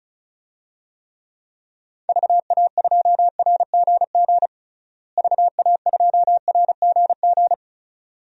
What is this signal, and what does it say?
Morse code, heard as a keyed tone, about 34 wpm, 700 Hz. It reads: VA2RGG VA2RGG
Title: DXCC-100 Call Signs - 6 Characters - Repeated 2x - 35wpm